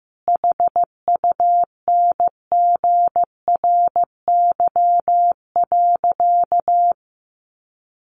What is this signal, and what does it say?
Morse code, heard as a keyed tone, about 15 wpm, 700 Hz. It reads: HUNGRY.
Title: Sentences from Top 1400 Words 15wpm